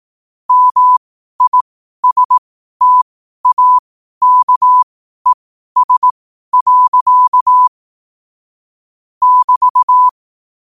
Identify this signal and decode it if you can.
Morse code, heard as a keyed tone, about 18 wpm, spaced as for 12 wpm, 1000 Hz. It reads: MISTAKES. <BT>
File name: qotdfiles / 12wpm